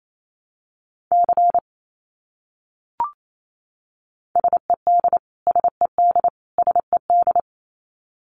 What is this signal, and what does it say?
Morse code, heard as a keyed tone, about 28 wpm, 700 Hz. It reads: C SED SED SED
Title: Abbreviations Repeated 3x - 28wpm